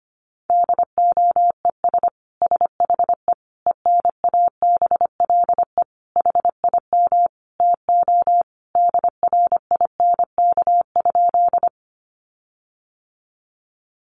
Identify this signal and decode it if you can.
Morse code, heard as a keyed tone, about 25 wpm, 700 Hz. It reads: DOES SHE ENABLE HIM TO DRINK?
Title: Sentences from Top 1500 Words 25wpm